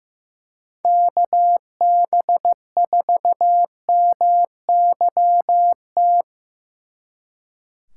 Morse code, heard as a keyed tone, about 18 wpm, 700 Hz. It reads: KB4MYT
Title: Speed Racing - USA 2x3 Call Signs 15wpm